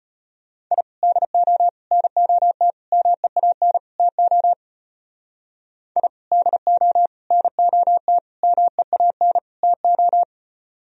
Morse code, heard as a keyed tone, about 32 wpm, 700 Hz. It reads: I DO NOT MEAN TO I DO NOT MEAN TO